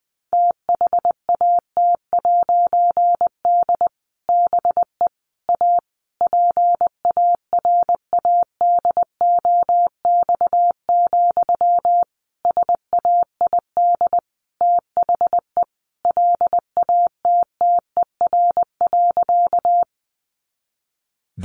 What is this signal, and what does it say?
Morse code, heard as a keyed tone, about 20 wpm, 700 Hz. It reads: THAT'D BE A PARADOX, SAID THE LATTER.
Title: A Sound Of Thunder 20wpm